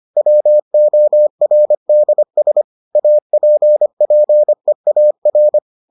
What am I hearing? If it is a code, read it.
Morse code, heard as a keyed tone, about 25 wpm, 600 Hz. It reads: WORDS APPEAR